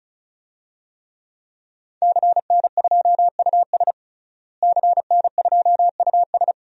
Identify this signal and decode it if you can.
Morse code, heard as a keyed tone, about 34 wpm, 700 Hz. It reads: CN2US CN2US
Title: DXCC-100 Call Signs - 5 Characters - Repeated 2x - 35wpm